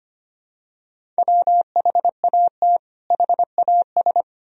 Morse code, heard as a keyed tone, about 25 wpm, 700 Hz. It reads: WHAT HAS
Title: Sentences from Top 100 Words and Abbreviations 25wpm